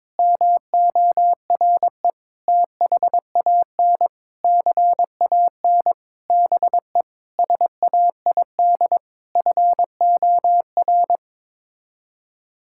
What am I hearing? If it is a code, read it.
Morse code, heard as a keyed tone, about 22 wpm, 700 Hz. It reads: MORE THAN CAN BE SAID FOR
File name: practice / 22wpm / Sets of 8 Words - 22wpm